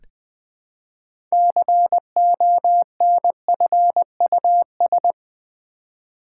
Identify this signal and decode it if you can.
Morse code, heard as a keyed tone, about 20 wpm, 700 Hz. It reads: CONFUS
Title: Build A Word - 3 Letters At A Time - 9-Letter Words - 20wpm